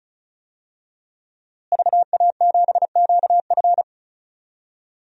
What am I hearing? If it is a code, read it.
Morse code, heard as a keyed tone, about 36 wpm, 700 Hz. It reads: VA7QF
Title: DXCC-100 Call Signs - 5 Characters - Repeated 3x - 35wpm